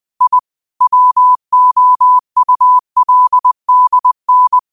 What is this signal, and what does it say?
Morse code, heard as a keyed tone, about 20 wpm, 1000 Hz. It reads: I WOULDN